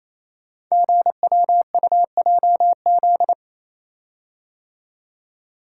Morse code, heard as a keyed tone, about 28 wpm, 700 Hz. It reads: GWUJZ
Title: Code Groups - 5 Letters - 28wpm